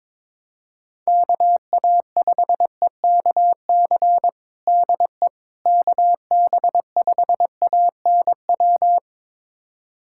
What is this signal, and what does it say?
Morse code, heard as a keyed tone, about 22 wpm, 700 Hz. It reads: KA5EKC DE KB5ANW
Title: USA Callsign DE Callsign - 6 Characters - 22wpm